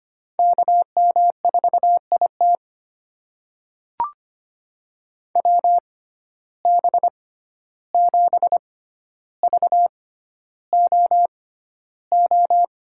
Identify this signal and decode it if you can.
Morse code, heard as a keyed tone, about 25 wpm, spaced as for 12 wpm, 700 Hz. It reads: KM4IT WB7VOO
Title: USA Call Signs - 6x Character Spacing - 25wpm